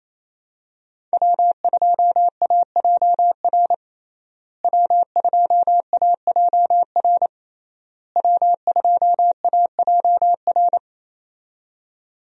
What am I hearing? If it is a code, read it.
Morse code, heard as a keyed tone, about 28 wpm, 700 Hz. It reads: W2AJR W2AJR W2AJR
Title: USA Call Signs - 5 Characters - Repeated 3x - 28wpm